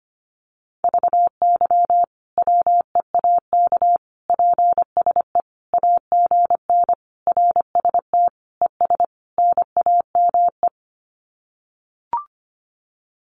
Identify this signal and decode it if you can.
Morse code, heard as a keyed tone, about 25 wpm, 700 Hz. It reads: VY WEAK PSE AGN RST ES NAME
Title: Common QSO Phrases 25wpm